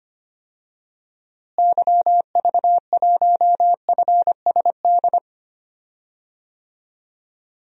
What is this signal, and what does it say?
Morse code, heard as a keyed tone, about 25 wpm, 700 Hz. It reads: YV1FSD
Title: DXCC-100 Call Signs - 6 Characters - Repeated 2x - 25wpm